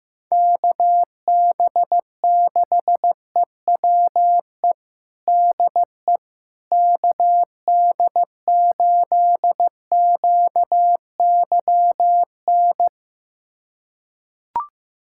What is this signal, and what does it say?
Morse code, heard as a keyed tone, about 15 wpm, 700 Hz. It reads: KB6EWE DE KD8QYN
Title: USA Callsign DE Callsign - 6 Characters - 15wpm